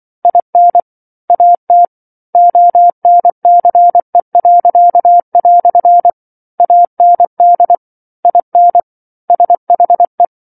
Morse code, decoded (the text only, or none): IN AT ONCE." AND IN SHE